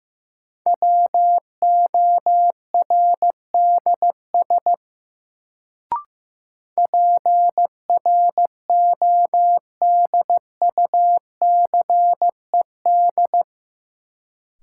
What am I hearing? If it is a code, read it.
Morse code, heard as a keyed tone, about 15 wpm, 700 Hz. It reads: WORDS PRODUCED
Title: Sentences from Top 1000 Words - Review Part 6 - New Word Drill - 15wpm